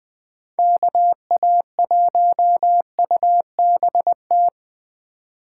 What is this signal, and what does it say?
Morse code, heard as a keyed tone, about 21 wpm, 700 Hz. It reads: KA1UBT